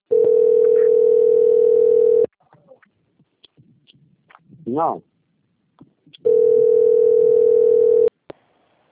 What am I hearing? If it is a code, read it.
Ringback tone (North America), 440 + 480 Hz.
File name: out- - - - - .70117